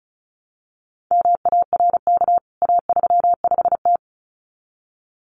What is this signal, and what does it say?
Morse code, heard as a keyed tone, about 34 wpm, 700 Hz. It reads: MARK A35T
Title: CWT Exchange 35wpm